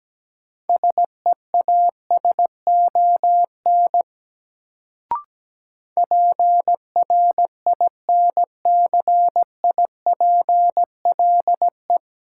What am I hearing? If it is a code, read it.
Morse code, heard as a keyed tone, about 17 wpm, 700 Hz. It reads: SEASON PRINCIPLE
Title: Sentences from Top 1400 Words - New Word Drill - 17wpm